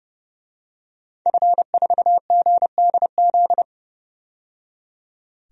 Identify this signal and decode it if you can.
Morse code, heard as a keyed tone, about 31 wpm, 700 Hz. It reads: F4GDZ
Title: DXCC-100 Call Signs - 5 Characters - 30wpm